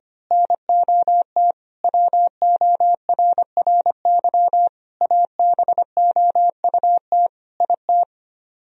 Morse code, heard as a keyed tone, about 24 wpm, 700 Hz. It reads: NOT WORRY ABOUT IT
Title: Speed-Racing - Sets of 5 Words 25wpm